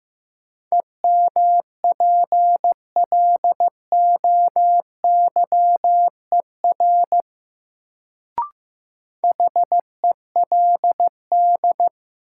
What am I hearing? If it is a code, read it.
Morse code, heard as a keyed tone, about 15 wpm, 700 Hz. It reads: EMPLOYER HELD
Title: Sentences from Top 1600 Words - New Word Drill - 15wpm